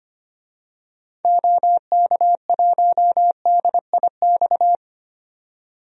Morse code, heard as a keyed tone, about 25 wpm, 700 Hz. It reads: OK1DIX
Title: DXCC-100 Call Signs - 6 Characters - 25wpm